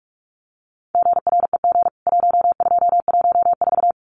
Morse code, heard as a keyed tone, about 44 wpm, 700 Hz. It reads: GREG 1214
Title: CWT Exchange 45wpm